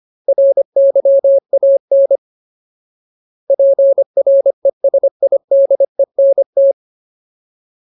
Morse code, heard as a keyed tone, about 25 wpm, 550 Hz. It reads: RYAN PRESIDENT
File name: Ultimate Word List / 25WPM Ultimate